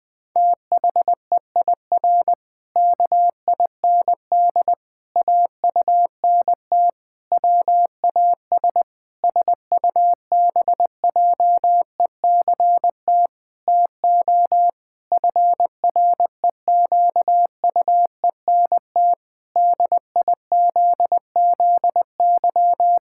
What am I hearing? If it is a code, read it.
Morse code, heard as a keyed tone, about 20 wpm, 700 Hz. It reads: THEIR KIND AUNT WAS SUBJECT TO FREQUENT DIZZY
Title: Pangram Challenge v2 - 20wpm